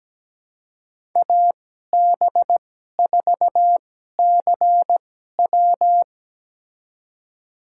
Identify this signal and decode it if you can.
Morse code, heard as a keyed tone, about 17 wpm, spaced as for 12 wpm, 700 Hz. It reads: AB4CW
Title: USA Call Signs - 5 Characters - 2x Character Spacing - 17wpm